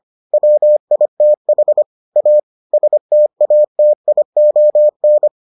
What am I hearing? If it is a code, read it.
Morse code, heard as a keyed tone, about 25 wpm, 600 Hz. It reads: WITH A STATION